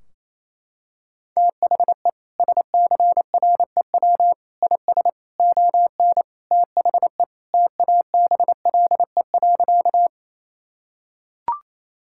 Morse code, heard as a keyed tone, about 28 wpm, 700 Hz. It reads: THE SCREW IS ON THE TABLE.
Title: Sentences from Top 1600 Words 28wpm